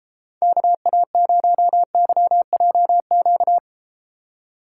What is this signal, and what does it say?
Morse code, heard as a keyed tone, about 33 wpm, 700 Hz. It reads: KA0YJQ